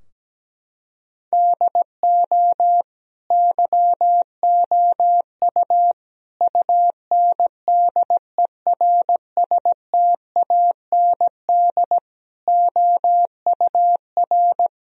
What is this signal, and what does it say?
Morse code, heard as a keyed tone, about 17 wpm, 700 Hz. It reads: DO YOU UNDERSTAND OUR
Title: Sentences from Top 1000 Words - Review Part 3 - 17wpm